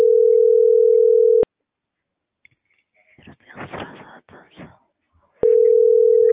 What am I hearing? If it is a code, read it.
Ringback tone (North America), 440 + 480 Hz.